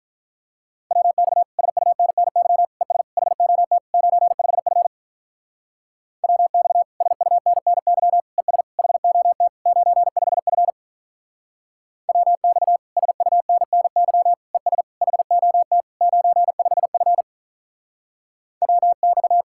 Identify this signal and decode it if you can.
Morse code, heard as a keyed tone, about 46 wpm, 700 Hz. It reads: WX SUNNY ES HOT 95F WX SUNNY ES HOT 95F WX SUNNY ES HOT 95F WX